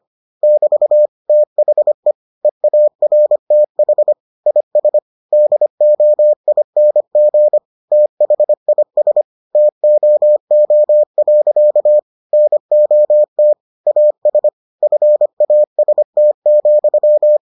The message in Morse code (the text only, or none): <BT> THE EARTH IS DOING THIS TOO. NOT AS FAST,